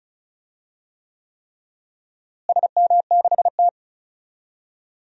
Morse code, heard as a keyed tone, about 36 wpm, 700 Hz. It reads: SM6T
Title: DXCC-100 Call Signs - 4 Characters - Repeated 3x - 35wpm